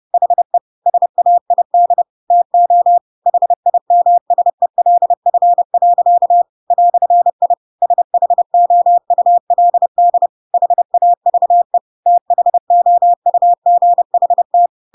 Morse code, heard as a keyed tone, about 30 wpm, 700 Hz. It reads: HE SAID TO HIMSELF. "I SHOULD HAVE THOUGHT
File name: Pinocchio / Pinocchio ch3 30 30 ik1hge